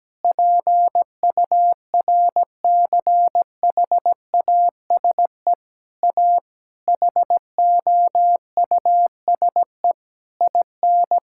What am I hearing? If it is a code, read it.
Morse code, heard as a keyed tone, about 17 wpm, 700 Hz. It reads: PURCHASE A HOUSE IN
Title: Sentences from Top 1000 Words - Review Part 5 - 17wpm